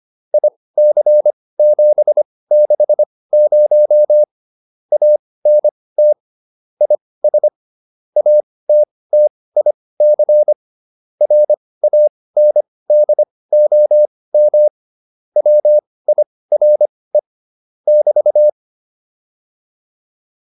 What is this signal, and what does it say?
Morse code, heard as a keyed tone, about 25 wpm, spaced as for 18 wpm, 600 Hz. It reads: IC760 ANT IS ATTIC RANDOM WIRE <BT>